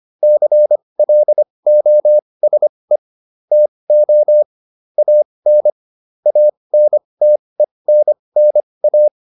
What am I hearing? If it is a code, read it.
Morse code, heard as a keyed tone, about 25 wpm, spaced as for 20 wpm, 600 Hz. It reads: CLOSE TO AN ANTENNA